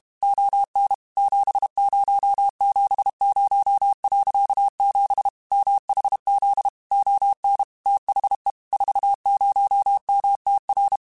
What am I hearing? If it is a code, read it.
Morse code, heard as a keyed tone, about 32 wpm, 800 Hz. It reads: ON 7070.7 MHZ ON THE 40MTR
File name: QSOs / QSOs-mp3s / kc7sac@30